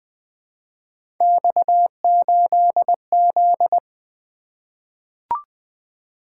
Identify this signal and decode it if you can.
Morse code, heard as a keyed tone, about 20 wpm, 700 Hz. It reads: X8Z
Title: Three Letter-Number Combinations - 20wpm